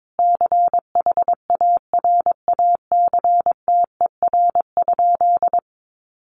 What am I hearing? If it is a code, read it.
Morse code, heard as a keyed tone, about 22 wpm, 700 Hz. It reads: CHARACTER?
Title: Sentences from Top 1200 Words 22wpm